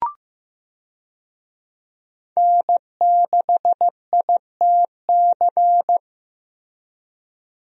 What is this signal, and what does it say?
Morse code, heard as a keyed tone, about 16 wpm, 700 Hz. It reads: N6ITC
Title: USA Call Signs - 5 Characters - Repeated 2x - 15wpm